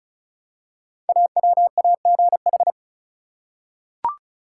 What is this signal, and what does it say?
Morse code, heard as a keyed tone, about 36 wpm, 700 Hz. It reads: AWAGH
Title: Code Groups - 5 Letter:Numbers - 35wpm